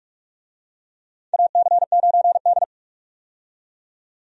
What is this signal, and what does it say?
Morse code, heard as a keyed tone, about 48 wpm, 700 Hz. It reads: AC9D
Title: USA Call Signs - 4 Characters - 45wpm